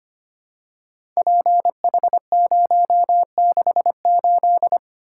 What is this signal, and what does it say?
Morse code, heard as a keyed tone, about 26 wpm, 700 Hz. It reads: PH068
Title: Code Groups - 5 Letter:Numbers - 25wpm